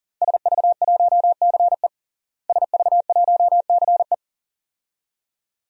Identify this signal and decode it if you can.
Morse code, heard as a keyed tone, about 40 wpm, 700 Hz. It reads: SV1CE SV1CE
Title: DXCC-100 Call Signs - 5 Characters - Repeated 2x - 40wpm